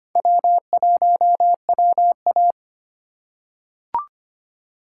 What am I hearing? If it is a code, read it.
Morse code, heard as a keyed tone, about 25 wpm, 700 Hz. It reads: W1WA
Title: USA Call Signs - 4 Characters - 2x Character Spacing - 25wpm